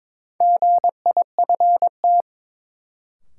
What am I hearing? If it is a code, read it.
Morse code, heard as a keyed tone, about 21 wpm, 700 Hz. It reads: GIFT
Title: Sentences from Top 900 Words - New Word Drill - 22wpm